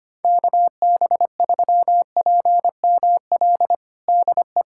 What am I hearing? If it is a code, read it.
Morse code, heard as a keyed tone, about 25 wpm, 700 Hz. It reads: KB3PML DE
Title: USA Callsign DE Callsign - 6 Characters - 25wpm